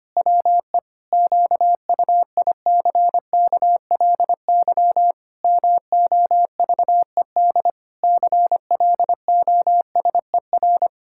Morse code, heard as a keyed tone, about 25 wpm, 700 Hz. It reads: WE QUICKLY MOVED CLOSER
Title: Pangram Challenge v2 - 25wpm